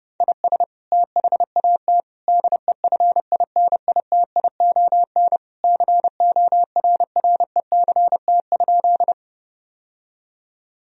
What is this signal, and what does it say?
Morse code, heard as a keyed tone, about 30 wpm, 700 Hz. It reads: IS THAT DEFINITION CORRECT?
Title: Sentences from Top 1000 Words - Review Part 4 - 30wpm